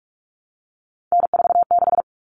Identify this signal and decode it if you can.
Morse code, heard as a keyed tone, about 44 wpm, 700 Hz. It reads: N46
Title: Common Cut Numbers - 3-Digit 45wpm